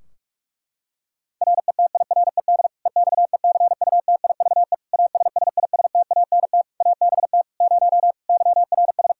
Morse code, heard as a keyed tone, about 42 wpm, 700 Hz. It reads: RETIRED EXECUTIVE ASSISTANT ABT 0 YRS
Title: K1USN SST QSO Practice 45wpm